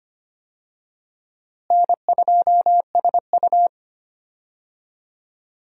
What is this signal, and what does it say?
Morse code, heard as a keyed tone, about 25 wpm, 700 Hz. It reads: N2SU
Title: USA Call Signs - 4 Characters - Repeated 3x - 25wpm